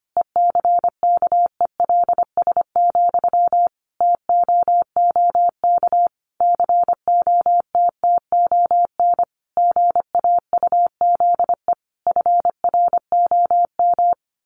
Morse code, heard as a keyed tone, about 25 wpm, 700 Hz. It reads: ECKELS, TOOK COTTON GAUZE FROM
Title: A Sound Of Thunder 25wpm